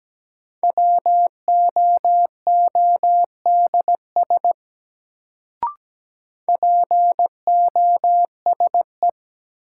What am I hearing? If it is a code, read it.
Morse code, heard as a keyed tone, about 17 wpm, 700 Hz. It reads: WOODS POSE
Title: Sentences from Top 1000 Words - Review Part 2 - New Word Drill - 17wpm